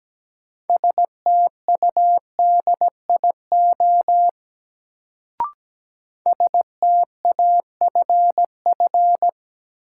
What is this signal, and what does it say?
Morse code, heard as a keyed tone, about 17 wpm, 700 Hz. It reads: STUDIO STAFF
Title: Sentences from Top 1200 Words - New Word Drill - 17wpm